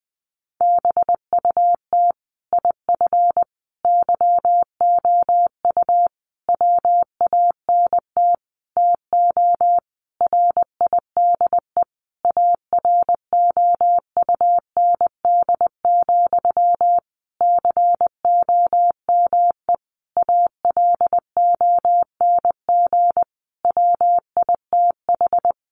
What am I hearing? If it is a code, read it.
Morse code, heard as a keyed tone, about 19 wpm, 700 Hz. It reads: BUT IF YOU WANT TO RIDE AROUND, COME ALONG WITH
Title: Sentences from Top 1200 Words 20wpm